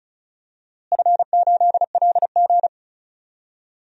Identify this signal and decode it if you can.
Morse code, heard as a keyed tone, about 34 wpm, 700 Hz. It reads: F8LG
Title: DXCC-100 Call Signs - 4 Characters - Repeated 2x - 35wpm